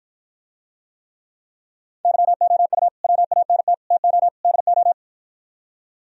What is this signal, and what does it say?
Morse code, heard as a keyed tone, about 52 wpm, 700 Hz. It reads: YOU WANT TO DO